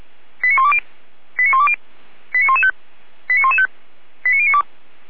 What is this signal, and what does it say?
Five-tone selective call (ZVEI), heard as quick runs of five tones, tones of about 70 ms. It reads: 89120 89120 89115 89115 89092